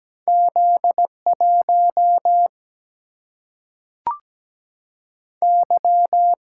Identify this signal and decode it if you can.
Morse code, heard as a keyed tone, about 17 wpm, 700 Hz. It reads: Z1 Y
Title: Single-Letter Single-Number 17wpm